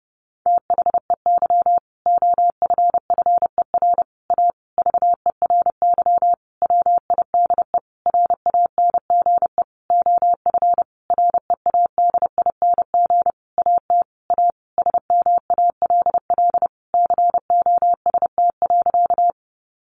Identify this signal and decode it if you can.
Morse code, heard as a keyed tone, about 30 wpm, 700 Hz. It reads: THEY OFFER A VERY WIDE RANGE OF READING AT A SMALL COST.
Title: Sentences from Top 900 Words 30wpm